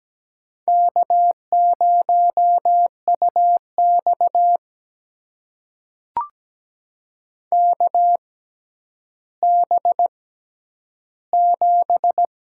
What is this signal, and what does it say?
Morse code, heard as a keyed tone, about 17 wpm, spaced as for 10 wpm, 700 Hz. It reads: K0UX KB7
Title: USA Call Signs - 6x Character Spacing - 17wpm